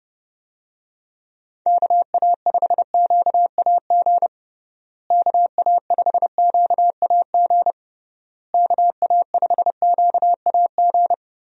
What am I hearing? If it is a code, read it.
Morse code, heard as a keyed tone, about 30 wpm, 700 Hz. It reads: KA5QAG KA5QAG KA5QAG
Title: USA Call Signs - 6 Characters - Repeated 3x - 30wpm